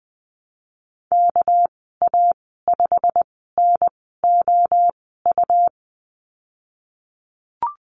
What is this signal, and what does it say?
Morse code, heard as a keyed tone, about 20 wpm, spaced as for 15 wpm, 700 Hz. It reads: KA5NOU
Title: 200 Lesson 32g - Letter k - Call Signs 20wpm